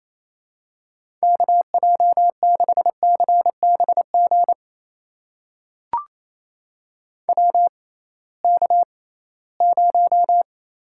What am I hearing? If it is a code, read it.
Morse code, heard as a keyed tone, about 28 wpm, spaced as for 18 wpm, 700 Hz. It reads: KJ6CBG WK0
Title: USA Call Signs - 6x Character Spacing - 28wpm